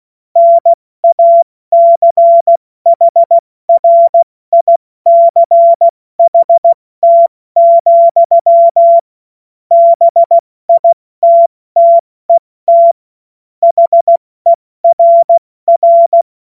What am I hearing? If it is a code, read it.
Morse code, heard as a keyed tone, about 16 wpm, 680 Hz. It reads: NACHRICHT, BITTET HERR
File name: Morsekurs / Witze / Witz 13. Juni 2023 70 F80